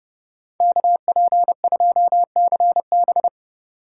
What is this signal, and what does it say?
Morse code, heard as a keyed tone, about 36 wpm, 700 Hz. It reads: KP2CB
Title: Speed Racing - USA 2x2 Call Signs 30wpm